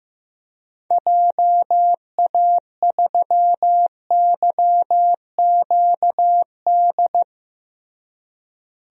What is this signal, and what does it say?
Morse code, heard as a keyed tone, about 16 wpm, 700 Hz. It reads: JA3YQD
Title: DXCC-100 Call Signs - 6 Characters - Repeated 3x - 15wpm